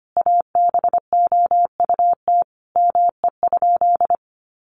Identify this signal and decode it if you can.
Morse code, heard as a keyed tone, about 25 wpm, 700 Hz. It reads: ABOUT ME?
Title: Sentences from Top 200 Words 25wpm